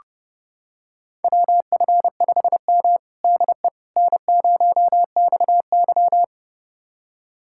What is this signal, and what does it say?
Morse code, heard as a keyed tone, about 31 wpm, 700 Hz. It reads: WF5M DE N0XY
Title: USA Callsign DE Callsign - 4 Characters - 30wpm